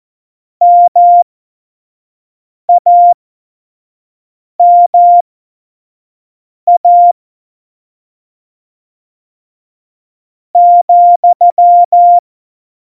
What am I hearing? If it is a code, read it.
Morse code, heard as a keyed tone, about 14 wpm, spaced as for 4 wpm, 700 Hz. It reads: MAMA,
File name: Morsekurs / Witze / Witz 04. September 2018, 25 F70